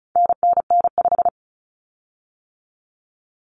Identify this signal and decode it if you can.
Morse code, heard as a keyed tone, about 34 wpm, 700 Hz. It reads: NNN5
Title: Common Cut Numbers - 4-Digit 35wpm